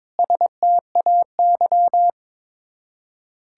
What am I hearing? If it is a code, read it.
Morse code, heard as a keyed tone, about 21 wpm, 700 Hz. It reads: STAY
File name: practice / 22wpm / Sets of 8 Words - 22wpm